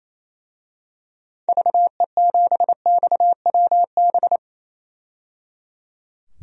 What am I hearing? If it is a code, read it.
Morse code, heard as a keyed tone, about 28 wpm, 700 Hz. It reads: VE7XWB
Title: DXCC-100 Call Signs - 6 Characters - 28wpm